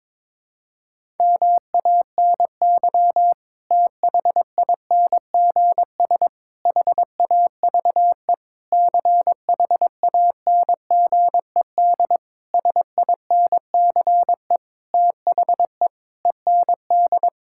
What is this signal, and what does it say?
Morse code, heard as a keyed tone, about 22 wpm, 700 Hz. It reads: MANY THINGS HAVE CHANGED SINCE THE END
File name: Sentences with 7-Word Phrases - 22wpm